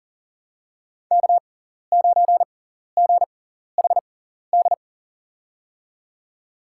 Morse code, heard as a keyed tone, about 40 wpm, spaced as for 14 wpm, 700 Hz. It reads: K9GHD
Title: USA Call Signs - 5 Characters - 6x Character Spacing - 40wpm